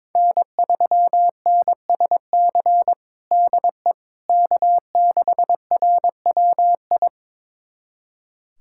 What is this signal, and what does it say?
Morse code, heard as a keyed tone, about 22 wpm, 700 Hz. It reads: N3NSC DE K6RWI
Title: USA Callsign DE Callsign - 5 Characters - 22wpm